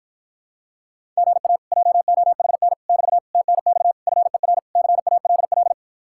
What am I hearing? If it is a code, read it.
Morse code, heard as a keyed tone, about 53 wpm, 700 Hz. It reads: GA JOHN <BT> TNX FER CALL